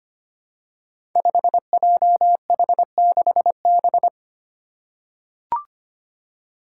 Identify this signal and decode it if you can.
Morse code, heard as a keyed tone, about 25 wpm, 700 Hz. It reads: 5JH6B
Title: Five Challenging Letter-Number Combinations 25wpm